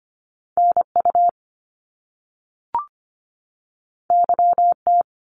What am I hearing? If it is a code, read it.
Morse code, heard as a keyed tone, about 25 wpm, 700 Hz. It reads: NU YT
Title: Canadian Provinces and Territories 25wpm